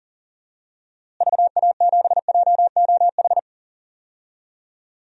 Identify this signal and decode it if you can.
Morse code, heard as a keyed tone, about 40 wpm, 700 Hz. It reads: VA7JOH
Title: DXCC-100 Call Signs - 6 Characters - Repeated 2x - 40wpm